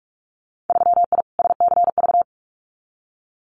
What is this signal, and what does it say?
Morse code, heard as a keyed tone, about 46 wpm, 700 Hz. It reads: 3I SCV
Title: Winter Field Day Exchanges Encore 45wpm